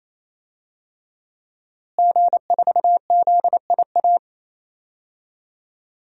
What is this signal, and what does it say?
Morse code, heard as a keyed tone, about 28 wpm, 700 Hz. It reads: G4ZIA
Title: DXCC-100 Call Signs - 5 Characters - Repeated 3x - 28wpm